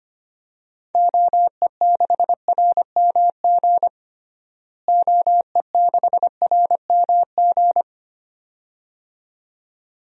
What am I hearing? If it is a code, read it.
Morse code, heard as a keyed tone, about 25 wpm, 700 Hz. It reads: OE6RMG OE6RMG
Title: DXCC-100 Call Signs - 6 Characters - Repeated 2x - 25wpm